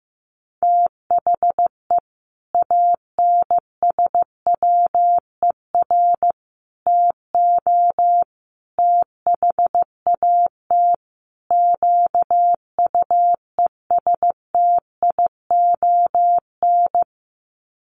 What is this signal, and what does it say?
Morse code, heard as a keyed tone, about 14 wpm, 700 Hz. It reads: THE ANSWER TO THAT QUESTION
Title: Sets of 5 Words Encore2 15wpm